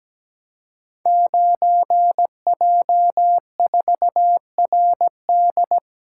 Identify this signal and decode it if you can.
Morse code, heard as a keyed tone, about 17 wpm, 700 Hz. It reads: 9J4RD
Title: Code Groups - 5 Letter:Numbers - 17wpm